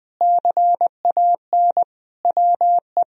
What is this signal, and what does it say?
Morse code, heard as a keyed tone, about 20 wpm, 700 Hz. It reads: CAN WE
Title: Sentences from Top 1000 Words - Review Part 4 - 20wpm